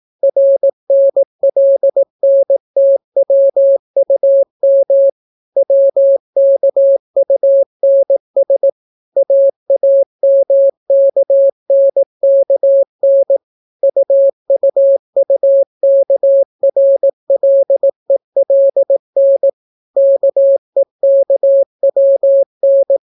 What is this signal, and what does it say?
Morse code, heard as a keyed tone, about 18 wpm, 550 Hz. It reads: RNLNTWUM WKUNS AAMKNKN UUUKRLELN KEKWN